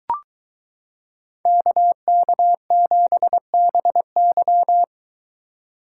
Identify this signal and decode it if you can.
Morse code, heard as a keyed tone, about 22 wpm, 700 Hz. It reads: KK7BY